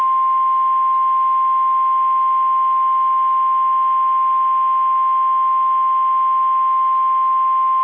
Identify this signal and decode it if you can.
Emergency Alert System (SAME) tones (the 1050 Hz attention tone). It is heard from 1.0 to 7.5 s.
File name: eas / KSDZKDJL / KSDZKDJL